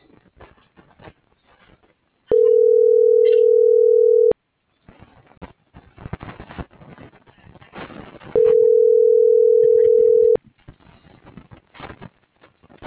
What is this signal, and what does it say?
Ringback tone (North America), 440 + 480 Hz.